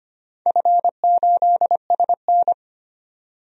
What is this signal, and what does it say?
Morse code, heard as a keyed tone, about 25 wpm, 700 Hz. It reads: F8SN
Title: DXCC-100 Call Signs - 4 Characters - 25wpm